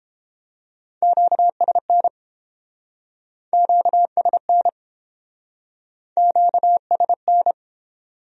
Morse code, heard as a keyed tone, about 29 wpm, 700 Hz. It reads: QSN QSN QSN